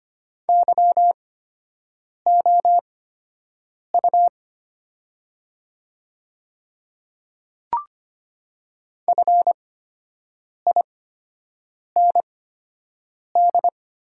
Morse code, heard as a keyed tone, about 25 wpm, spaced as for 7 wpm, 700 Hz. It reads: YOU FIND
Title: 8x Character Spacing - Top 100 Words - 25wpm